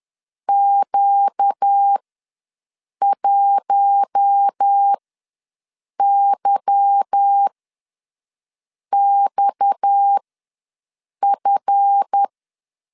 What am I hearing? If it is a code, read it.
Morse code, heard as a keyed tone, about 10 wpm, spaced as for 6 wpm, 800 Hz. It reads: Q1YXF
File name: LEZIONI / 9 286 ws 10wpm 4032